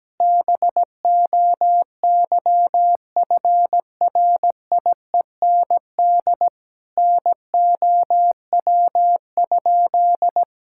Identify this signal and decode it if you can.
Morse code, heard as a keyed tone, about 17 wpm, 700 Hz. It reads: BOYFRIEND NOW?
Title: Sentences from Top 1000 Words - Review Part 5 - 17wpm